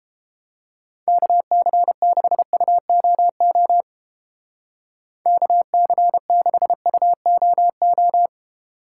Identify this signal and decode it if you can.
Morse code, heard as a keyed tone, about 29 wpm, 700 Hz. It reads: KC6UOO KC6UOO